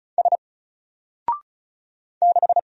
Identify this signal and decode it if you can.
Morse code, heard as a keyed tone, about 34 wpm, 700 Hz. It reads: S 6
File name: Single-Letter-Number-35wpm